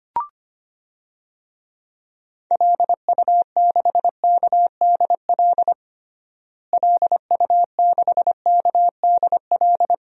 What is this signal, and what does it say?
Morse code, heard as a keyed tone, about 25 wpm, 700 Hz. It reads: LU6KDL LU6KDL
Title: DXCC-100 Call Signs - 6 Characters - Repeated 3x - 25wpm